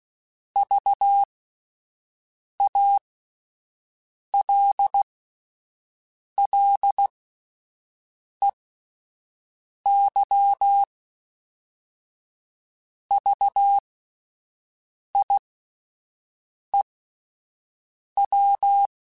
Morse code, heard as a keyed tone, about 16 wpm, spaced as for 5 wpm, 800 Hz. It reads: VALLEY VIEW